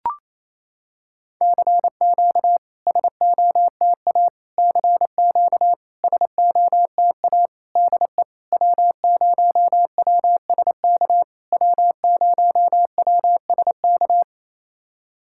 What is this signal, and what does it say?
Morse code, heard as a keyed tone, about 27 wpm, 700 Hz. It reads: CQ SOTA CQ SOTA DE W0WSK W0WSK
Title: SOTA QSO Practice 28wpm